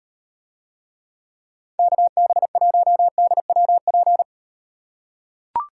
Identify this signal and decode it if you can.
Morse code, heard as a keyed tone, about 38 wpm, 700 Hz. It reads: KB1DWP